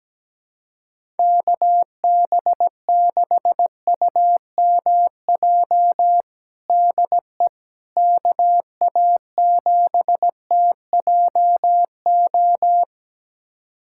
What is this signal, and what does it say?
Morse code, heard as a keyed tone, about 17 wpm, 700 Hz. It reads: KB6UMJ DE KA7TJO
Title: USA Callsign DE Callsign - 6 Characters - 17wpm